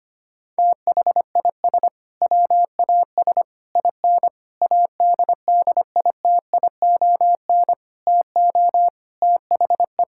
Morse code, heard as a keyed tone, about 25 wpm, 700 Hz. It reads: THIS WAS IN ADDITION TO THE
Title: Sentences from Top 1000 Words - Review Part 6 - 25wpm